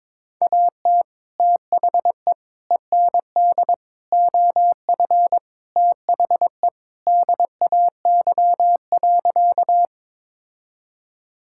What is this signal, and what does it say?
Morse code, heard as a keyed tone, about 22 wpm, 700 Hz. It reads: AT THE END OF THE DAY.
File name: practice / 22wpm / Sentences with 3-Word Phrases - 22wpm